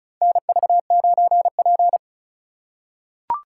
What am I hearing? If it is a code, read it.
Morse code, heard as a keyed tone, about 36 wpm, 700 Hz. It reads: NV9P
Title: USA Call Signs - 4x Character Spacing - 35wpm